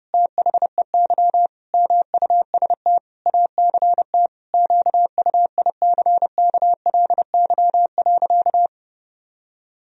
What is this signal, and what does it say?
Morse code, heard as a keyed tone, about 28 wpm, 700 Hz. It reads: THEY MUST ACT QUICKLY.
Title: Sentences from Top 1000 Words - Review Part 4 - 30wpm